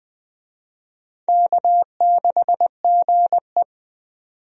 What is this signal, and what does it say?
Morse code, heard as a keyed tone, about 20 wpm, 700 Hz. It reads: K6GE
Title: USA Call Signs - 4 Characters - 20wpm